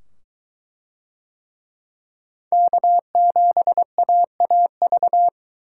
Morse code, heard as a keyed tone, about 22 wpm, 700 Hz. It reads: K7AAV